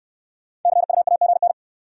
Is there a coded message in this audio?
Yes: Morse code from 0.6 to 1.5 s, heard as a keyed tone, about 64 wpm, 700 Hz. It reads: CUAGN